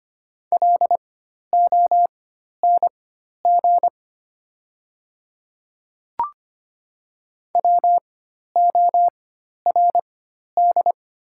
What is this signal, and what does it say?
Morse code, heard as a keyed tone, about 25 wpm, spaced as for 12 wpm, 700 Hz. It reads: LONG WORD
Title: 4x Character Spacing - Top 200 Words - 25wpm